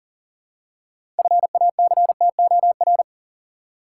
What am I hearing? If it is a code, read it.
Morse code, heard as a keyed tone, about 40 wpm, 700 Hz. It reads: FACTOR
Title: Six Letter Words - 40wpm